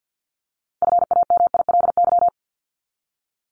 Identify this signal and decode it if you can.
Morse code, heard as a keyed tone, about 50 wpm, 700 Hz. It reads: FAMILY
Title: Top 500 Words 50wpm